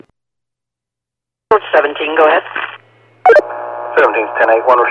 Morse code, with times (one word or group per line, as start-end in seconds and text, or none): none